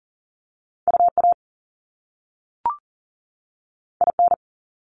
Morse code, heard as a keyed tone, about 41 wpm, 700 Hz. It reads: UA IN
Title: Top Two Letter Combinations in Words 40wpm